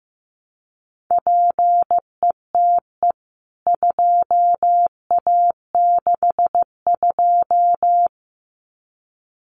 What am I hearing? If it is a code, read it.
Morse code, heard as a keyed tone, about 15 wpm, 700 Hz. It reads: PETE 2A62
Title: CWT Exchange 15wpm